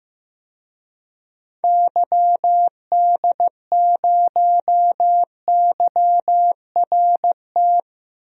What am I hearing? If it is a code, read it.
Morse code, heard as a keyed tone, about 15 wpm, 700 Hz. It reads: YD0YRT
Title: DXCC-100 Call Signs - 6 Characters - Repeated 2x - 15wpm